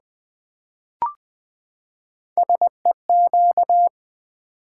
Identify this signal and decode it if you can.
Morse code, heard as a keyed tone, about 20 wpm, 700 Hz. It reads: SEQ
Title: Build A Word - 3 Letters At A Time - 8-Letter Words - 20wpm